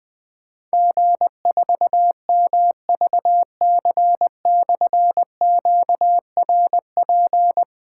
Morse code, heard as a keyed tone, about 20 wpm, 700 Hz. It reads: G4MVC/QRP
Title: Difficult Call Signs 20wpm